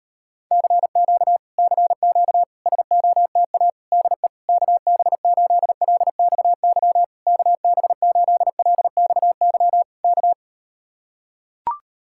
Morse code, heard as a keyed tone, about 39 wpm, 700 Hz. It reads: CQ CQ SOTA DE KB8LXY KB8LXY K